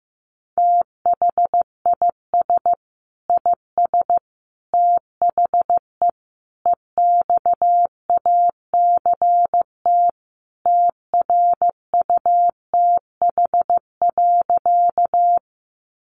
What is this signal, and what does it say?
Morse code, heard as a keyed tone, about 14 wpm, 700 Hz. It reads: THIS IS THE EXACT TRUTH.
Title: Sentences from Top 1300 Words 15wpm